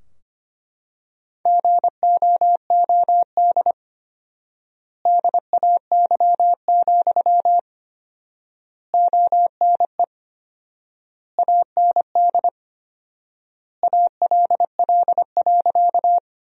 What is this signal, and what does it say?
Morse code, heard as a keyed tone, about 25 wpm, 700 Hz. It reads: GOOD DAY, ONE AND ALL.
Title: 4x Word-Spacing - Sentences from Top 200 Words - 25wpm